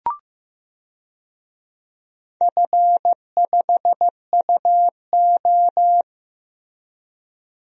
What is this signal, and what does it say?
Morse code, heard as a keyed tone, about 15 wpm, 700 Hz. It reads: F5UO
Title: DXCC-100 Call Signs - 4 Characters - Repeated 2x - 15wpm